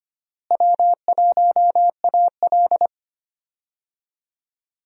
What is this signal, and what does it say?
Morse code, heard as a keyed tone, about 25 wpm, 700 Hz. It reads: W1AL
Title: USA Call Signs - 4 Characters - Repeated 2x - 25wpm